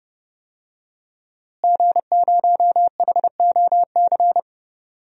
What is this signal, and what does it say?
Morse code, heard as a keyed tone, about 30 wpm, 700 Hz. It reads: G0HOC
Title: DXCC-100 Call Signs - 5 Characters - Repeated 3x - 30wpm